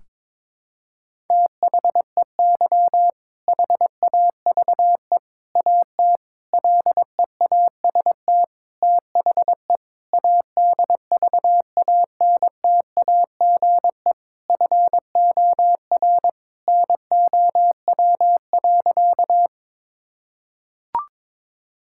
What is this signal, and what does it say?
Morse code, heard as a keyed tone, about 22 wpm, 700 Hz. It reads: THEY HAVE AT LEAST THE ADVANTAGE FOR NOW.
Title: Sentences from Top 1000 Words - Review Part 6 - 22wpm